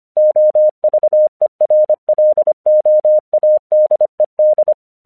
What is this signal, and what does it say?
Morse code, heard as a keyed tone, about 25 wpm, 610 Hz. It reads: OVERLOADED
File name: 1 English 25wpm / AQC06